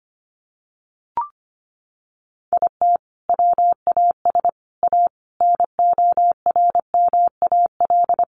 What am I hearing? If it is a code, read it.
Morse code, heard as a keyed tone, about 25 wpm, 700 Hz. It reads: IT WAS A NORMAL
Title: Sentences from Top 1200 Words 25wpm